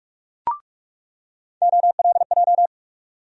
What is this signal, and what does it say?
Morse code, heard as a keyed tone, about 47 wpm, 700 Hz. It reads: OLJ
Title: USA Call Sign Postfixes - 3 Letters 45wpm